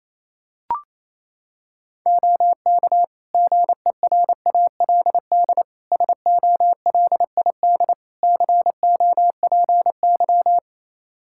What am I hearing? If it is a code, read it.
Morse code, heard as a keyed tone, about 28 wpm, 700 Hz. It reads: OK GERALD SOLID COPY
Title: Practice QSO - Exchange 3 - More Info - 28wpm